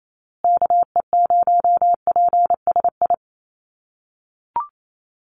Morse code, heard as a keyed tone, about 28 wpm, 700 Hz. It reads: KE0PSI
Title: USA Call Signs Repeated 2x 28wpm